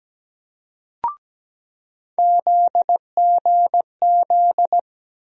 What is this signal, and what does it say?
Morse code, heard as a keyed tone, about 17 wpm, 700 Hz. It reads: ZGZ
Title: USA Call Sign Postfixes - 3 Letters 17wpm